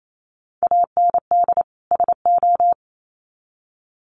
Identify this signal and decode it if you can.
Morse code, heard as a keyed tone, about 28 wpm, 700 Hz. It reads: AND SO
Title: Sets of 2 Words 28wpm